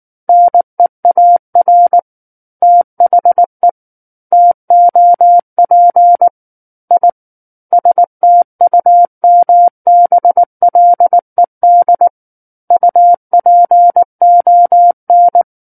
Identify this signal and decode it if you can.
Morse code, heard as a keyed tone, about 19 wpm, 700 Hz. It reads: NEAR THE TOP I STUMBLED UPON
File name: Book One / 10-Chapter 10 - 19 WPM